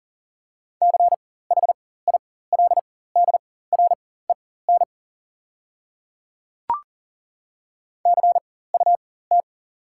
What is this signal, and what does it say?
Morse code, heard as a keyed tone, about 40 wpm, spaced as for 19 wpm, 700 Hz. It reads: CHILDREN CUT
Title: 4x Character Spacing - Top 300 Words - 40wpm